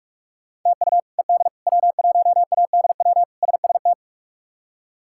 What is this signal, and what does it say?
Morse code, heard as a keyed tone, about 47 wpm, 700 Hz. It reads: TU ED W1ADW SST
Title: K1USN SST QSO Practice - 45wpm